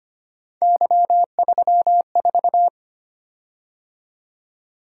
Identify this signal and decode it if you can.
Morse code, heard as a keyed tone, about 25 wpm, 700 Hz. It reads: Y34
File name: Three Letter-Number Combinations - 25wpm